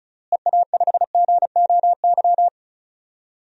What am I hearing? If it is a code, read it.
Morse code, heard as a keyed tone, about 34 wpm, 700 Hz. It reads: EA5GOY
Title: DXCC-100 Call Signs - 6 Characters - Repeated 3x - 35wpm